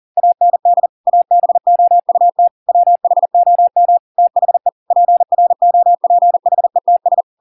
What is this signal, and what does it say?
Morse code, heard as a keyed tone, about 40 wpm, 700 Hz. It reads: AND ABOUT WHOM THE PROPHETS